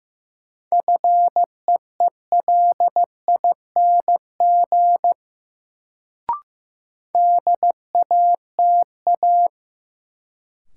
Morse code, heard as a keyed tone, about 15 wpm, 700 Hz. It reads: FEELING DATA
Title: Sentences from Top 1000 Words - Review Part 3 - New Word Drill - 15wpm